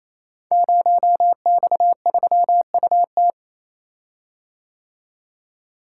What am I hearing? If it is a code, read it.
Morse code, heard as a keyed tone, about 27 wpm, 700 Hz. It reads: 0X3UT
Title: Code Groups - 5 Letter:Numbers - 28wpm